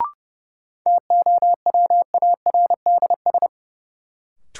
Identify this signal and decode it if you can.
Morse code, heard as a keyed tone, about 30 wpm, 700 Hz. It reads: TOWARDS
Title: Sentences from Top 1000 Words - Review Part 2 - New Word Drill - 30wpm